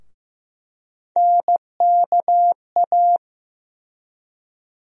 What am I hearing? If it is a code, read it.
Morse code, heard as a keyed tone, about 15 wpm, 700 Hz. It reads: NKA
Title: USA Call Sign Postfixes - 3 Letters 15wpm